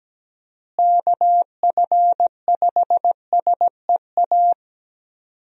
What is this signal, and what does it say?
Morse code, heard as a keyed tone, about 17 wpm, 700 Hz. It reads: KF5SEA
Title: USA Call Signs - 6 Characters - Repeated 3x - 17wpm